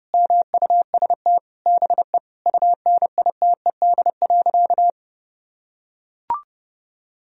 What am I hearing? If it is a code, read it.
Morse code, heard as a keyed tone, about 30 wpm, 700 Hz. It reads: MUST BE UNITED.
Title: Sentences from Top 1000 Words - Review Part 4 - 30wpm